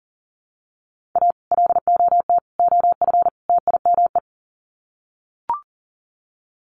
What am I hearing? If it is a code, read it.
Morse code, heard as a keyed tone, about 38 wpm, 700 Hz. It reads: A LOT OF TIME
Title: Sets of 4 Words 40wpm